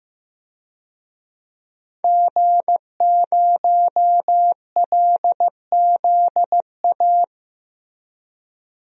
Morse code, heard as a keyed tone, about 15 wpm, 700 Hz. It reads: G0LZA
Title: DXCC-100 Call Signs - 5 Characters - Repeated 2x - 15wpm